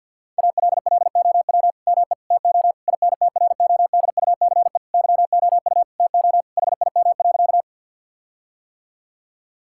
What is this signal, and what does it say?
Morse code, heard as a keyed tone, about 50 wpm, 700 Hz. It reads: ALLOW ME TO INTRODUCE YOU TO HIM.
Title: Sentences from Top 1600 Words 50wpm